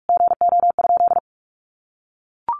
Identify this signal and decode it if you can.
Morse code, heard as a keyed tone, about 44 wpm, 700 Hz. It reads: GO?
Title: Sentences from Top 1300 Words 45wpm